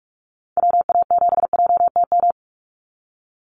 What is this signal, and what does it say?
Morse code, heard as a keyed tone, about 44 wpm, 700 Hz. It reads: WA7JTM
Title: NA SOTA Activator Callsigns 45wpm